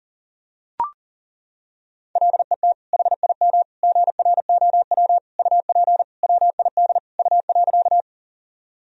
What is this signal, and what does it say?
Morse code, heard as a keyed tone, about 37 wpm, 700 Hz. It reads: LET HIM GROW UP WID U.
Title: Sentences from Top 300 Words and Abbreviations 40wpm